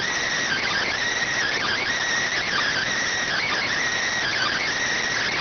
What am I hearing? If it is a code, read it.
Emergency Alert System (SAME) tones (header bursts), heard as harsh buzzing bursts. It is heard from 0.4 to 4.7 s.
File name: findings / CT SAGITAL 27.gen-18 rx